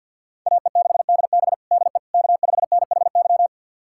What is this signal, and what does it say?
Morse code, heard as a keyed tone, about 47 wpm, 700 Hz. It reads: AE6DB DE K5NHY
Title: USA Callsign DE Callsign - 5 Characters - 50wpm